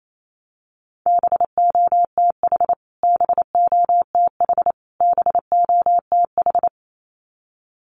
Morse code, heard as a keyed tone, about 28 wpm, 700 Hz. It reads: BOTH BOTH BOTH
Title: Top 500 Words Repeated 28wpm